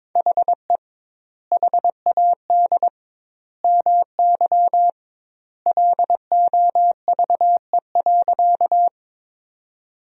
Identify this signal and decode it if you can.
Morse code, heard as a keyed tone, about 22 wpm, 700 Hz. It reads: HE HAD MY LOVE.
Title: 2x Word-Spacing - Sentences from Top 400 Words - 22wpm